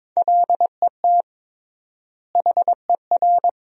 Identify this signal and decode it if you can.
Morse code, heard as a keyed tone, about 21 wpm, 700 Hz. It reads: LET HER
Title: 3x Word-Spacing - Sentences from Top 300 Words - 22wpm